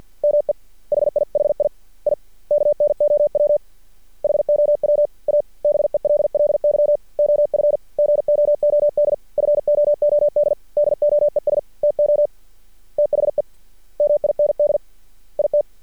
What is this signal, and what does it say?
Morse code, heard as a keyed tone, about 50 wpm, 590 Hz. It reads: ME HIHI I KNOW HOW A BELLY OF GOOD FOOD DOES TO THE MIND IT
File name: archived / CRSnet